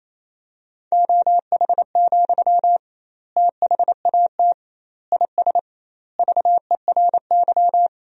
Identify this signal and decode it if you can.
Morse code, heard as a keyed tone, about 28 wpm, 700 Hz. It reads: OH, THAT IS VERY
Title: 2x Word-Spacing - Sentences from Top 500 Words - 28wpm